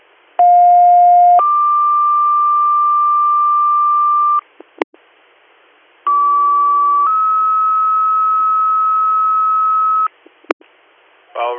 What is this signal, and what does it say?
Two-tone page, heard as a one-second tone, then a three-second tone, about 710 then 1180 Hz, about 1150 then 1290 Hz. It is heard from 0.4 to 10.1 s.